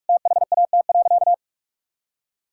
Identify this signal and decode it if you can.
Morse code, heard as a keyed tone, about 46 wpm, 700 Hz. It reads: THAT.
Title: Sentences from Top 1600 Words 45wpm